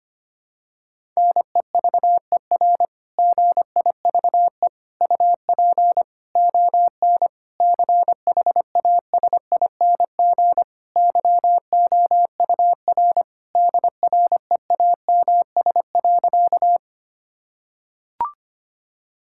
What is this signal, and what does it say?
Morse code, heard as a keyed tone, about 25 wpm, 700 Hz. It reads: NEVER GIVE UP ON CHASING YOUR DREAMS.
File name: practice / 25wpm / Difficult Sentences with Hints - 25wpm